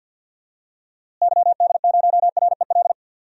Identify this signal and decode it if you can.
Morse code, heard as a keyed tone, about 53 wpm, 700 Hz. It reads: YD0REL
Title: DXCC-100 Call Signs - 6 Characters - Repeated 2x - 50wpm